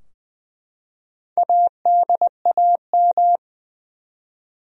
Morse code, heard as a keyed tone, about 20 wpm, 700 Hz. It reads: ADAM
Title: USA Common Names - 20wpm